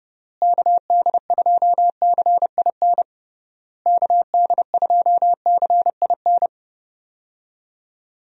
Morse code, heard as a keyed tone, about 31 wpm, 700 Hz. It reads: KD2CIN KD2CIN
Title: USA Call Signs - 6 Characters - Repeated 3x - 30wpm